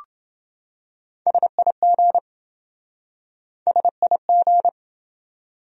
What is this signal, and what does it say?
Morse code, heard as a keyed tone, about 27 wpm, 700 Hz. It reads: SIG SIG